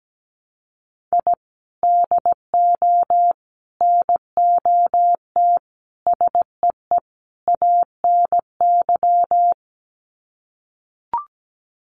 Morse code, heard as a keyed tone, about 17 wpm, 700 Hz. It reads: I DO NOT SEE ANY
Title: Sets of 5 Words 17wpm